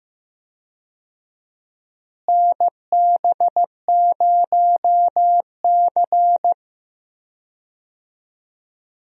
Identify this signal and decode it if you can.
Morse code, heard as a keyed tone, about 15 wpm, 700 Hz. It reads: NB0C
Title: USA Call Signs - 4 Characters - Repeated 3x - 15wpm